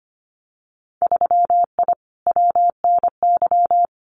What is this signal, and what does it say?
Morse code, heard as a keyed tone, about 25 wpm, 700 Hz. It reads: 3I WNY
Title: Winter Field Day Exchanges Encore 25wpm